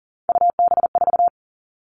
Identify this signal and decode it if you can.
Morse code, heard as a keyed tone, about 40 wpm, 700 Hz. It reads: UB4
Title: All Cut Numbers - 3-Digit 40wpm